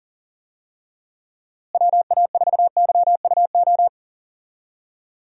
Morse code, heard as a keyed tone, about 40 wpm, 700 Hz. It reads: WA4YUO
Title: USA Call Signs - 6 Characters - Repeated 2x - 40wpm